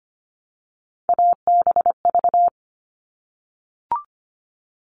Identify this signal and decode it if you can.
Morse code, heard as a keyed tone, about 25 wpm, 700 Hz. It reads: ABV
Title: All Cut Numbers - 3-Digit 25wpm